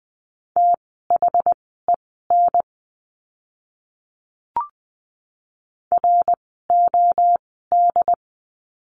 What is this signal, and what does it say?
Morse code, heard as a keyed tone, about 20 wpm, spaced as for 14 wpm, 700 Hz. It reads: THEN ROD
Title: 068 Lesson 12f - Letter d - Words 20wpm